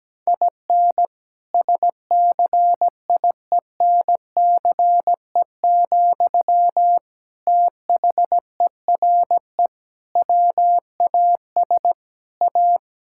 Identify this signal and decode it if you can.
Morse code, heard as a keyed tone, about 17 wpm, 700 Hz. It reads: IN SCIENCE, THERE WAS A
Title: Sentences from Top 1000 Words - Review Part 5 - 17wpm